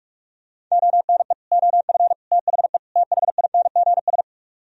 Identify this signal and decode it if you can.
Morse code, heard as a keyed tone, about 46 wpm, 700 Hz. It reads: ONE OF THE THINGS